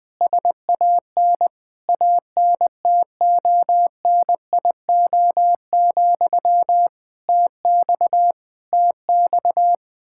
Morse code, heard as a keyed tone, about 20 wpm, 700 Hz. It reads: SAN ANTONIO, TX TX
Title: K1USN SST QSO Practice 20wpm